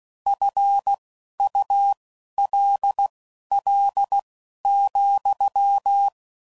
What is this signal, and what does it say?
Morse code, heard as a keyed tone, about 16 wpm, spaced as for 11 wpm, 800 Hz. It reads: FULL,